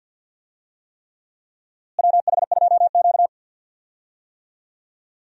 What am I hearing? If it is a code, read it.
Morse code, heard as a keyed tone, about 46 wpm, 700 Hz. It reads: WH2Q
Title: USA Call Signs - 4 Characters - Repeated 3x - 50wpm